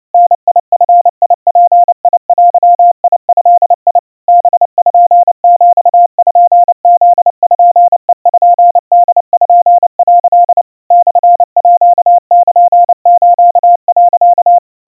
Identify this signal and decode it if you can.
Morse code (Japanese Wabun code), heard as a keyed tone, about 29 wpm, 700 Hz. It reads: ダヂヅデド パピプペポ」 モールス、
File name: aiueo 100 cpm 700 Hz Crispy